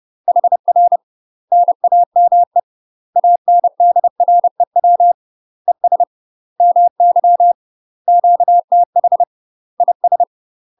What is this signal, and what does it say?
Morse code, heard as a keyed tone, about 30 wpm, 700 Hz. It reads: HR NAME ANDREW ES MY QTH IS
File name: UK MASTER QSOs TRAINING / 30WPM / UK MASTER QSOs 0039